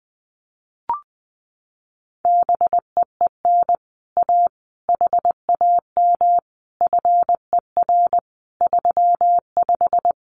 Morse code, heard as a keyed tone, about 20 wpm, 700 Hz. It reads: BEEN A HAM FER 35
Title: Common QSO Phrases Encore 20wpm